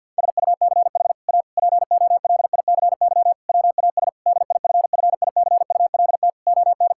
Morse code, heard as a keyed tone, about 50 wpm, 700 Hz. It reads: SUCH A POLICY WAS DIFFICULT ON
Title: Sentences from Top 1000 Words - Review Part 5 - 50wpm